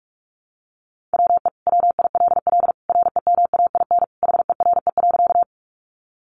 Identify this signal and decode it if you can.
Morse code, heard as a keyed tone, about 44 wpm, 700 Hz. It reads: WE WILL REMAIN HERE.
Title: Sentences from Top 1200 Words 45wpm